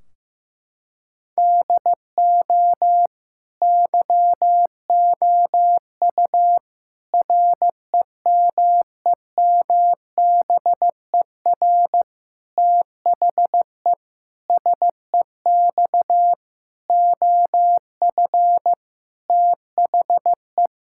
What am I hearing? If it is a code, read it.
Morse code, heard as a keyed tone, about 15 wpm, 700 Hz. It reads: DO YOU REMEMBER THE SEX OF THE
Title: Sentences from Top 1600 Words 15wpm